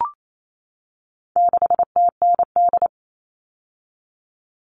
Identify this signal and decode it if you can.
Morse code, heard as a keyed tone, about 28 wpm, 700 Hz. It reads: 6TND
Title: All Cut Numbers - 4-Digit 28wpm